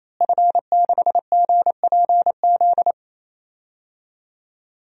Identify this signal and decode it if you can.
Morse code, heard as a keyed tone, about 28 wpm, 700 Hz. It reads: F6GPZ
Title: DXCC-100 Call Signs - 5 Characters - Repeated 3x - 28wpm